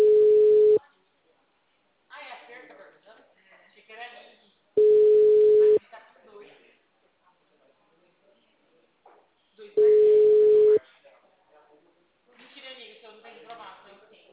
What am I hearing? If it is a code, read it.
Ringback tone (Europe, 425 Hz), with speech.
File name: out- -2011- - - .3217